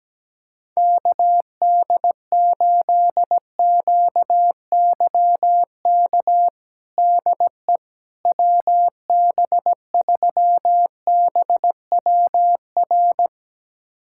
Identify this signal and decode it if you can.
Morse code, heard as a keyed tone, about 17 wpm, 700 Hz. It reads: KD8QYK DE WB3BWR
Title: USA Callsign DE Callsign 17wpm